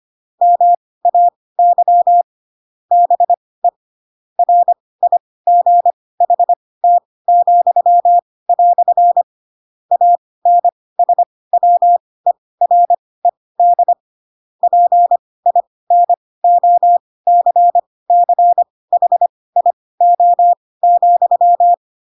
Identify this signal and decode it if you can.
Morse code, heard as a keyed tone, about 25 wpm, spaced as for 18 wpm, 700 Hz. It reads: MAY BE RIGHT," ANSWERED PINOCCHIO,
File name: Pinocchio / Pinocchio ch7 12 25 ik1hge